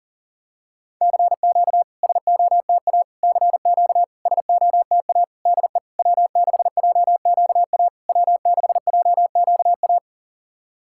Morse code, heard as a keyed tone, about 40 wpm, 690 Hz. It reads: CQ SOTA CQ SOTA DE W6JQA W6JQA
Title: SOTA QSO Practice 40wpm